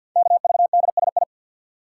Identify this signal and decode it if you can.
Morse code, heard as a keyed tone, about 48 wpm, 700 Hz. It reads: KVDSI
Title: Code Groups - 5 Letters - 50wpm